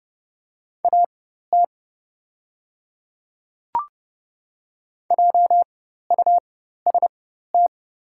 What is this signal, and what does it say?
Morse code, heard as a keyed tone, about 29 wpm, spaced as for 14 wpm, 700 Hz. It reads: AT JUST
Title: 4x Character Spacing - Top 200 Words - 30wpm